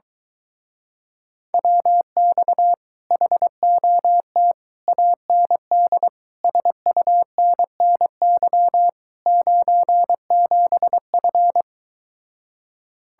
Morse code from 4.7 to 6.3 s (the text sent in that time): AND